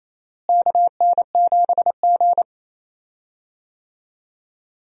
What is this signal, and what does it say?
Morse code, heard as a keyed tone, about 28 wpm, 700 Hz. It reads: KN7G
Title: USA Call Signs - 4 Characters - Repeated 2x - 28wpm